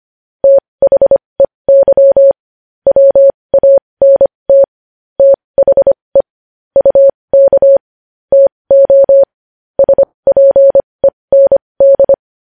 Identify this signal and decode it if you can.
Morse code, heard as a keyed tone, about 25 wpm, spaced as for 20 wpm, 550 Hz. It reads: THEY WANT THE UK TO SPEND
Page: news-headlines-in-morse-code-at-20-wpm